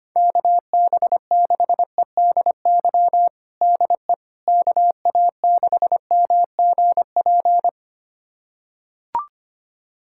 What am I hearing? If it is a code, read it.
Morse code, heard as a keyed tone, about 25 wpm, 700 Hz. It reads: KB6EDY DE KA6MGP
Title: USA Callsign DE Callsign - 6 Characters - 25wpm